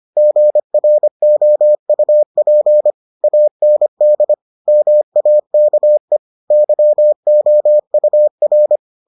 Morse code, heard as a keyed tone, about 25 wpm, 600 Hz. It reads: GROUP AND MAKE YOUR